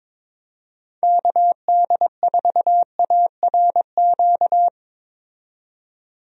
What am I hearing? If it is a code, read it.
Morse code, heard as a keyed tone, about 22 wpm, 700 Hz. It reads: KD4ARQ
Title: USA Call Signs - 6 Characters - 22wpm